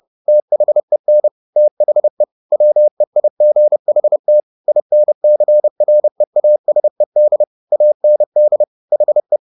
Morse code, heard as a keyed tone, about 30 wpm, 600 Hz. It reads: THEN THE WEIGHT INCREASED AND HE
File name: the old man and the sea / the old man and the sea 1 30wpm